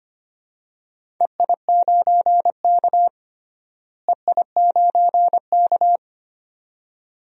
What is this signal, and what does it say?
Morse code, heard as a keyed tone, about 25 wpm, 700 Hz. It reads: EI9K EI9K
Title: DXCC-100 Call Signs - 4 Characters - Repeated 2x - 25wpm